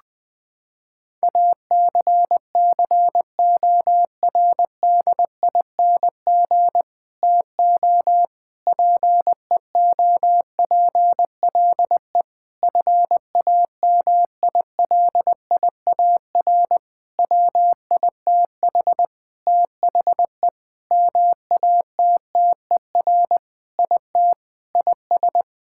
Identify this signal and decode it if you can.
Morse code, heard as a keyed tone, about 20 wpm, 700 Hz. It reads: ACCORDING TO PEOPLE FAMILIAR WITH THE MATTER IT IS
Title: Sentences with 7-Word Phrases - 20wpm